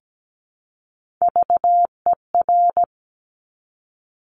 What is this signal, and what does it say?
Morse code, heard as a keyed tone, about 17 wpm, 700 Hz. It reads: VER
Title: Top Three Letter Combinations in Words 17wpm